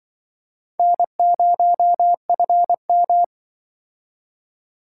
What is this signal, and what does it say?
Morse code, heard as a keyed tone, about 25 wpm, 700 Hz. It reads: N0FM
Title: Speed Racing - USA 1x2 Call Signs 20wpm